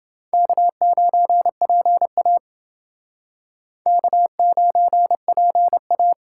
Morse code, heard as a keyed tone, about 28 wpm, 700 Hz. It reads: K9PA K9PA